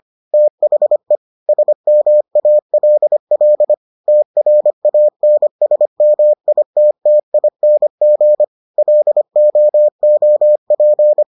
Morse code, heard as a keyed tone, about 25 wpm, 600 Hz. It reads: THE SMALL TRANSMITTING LOOP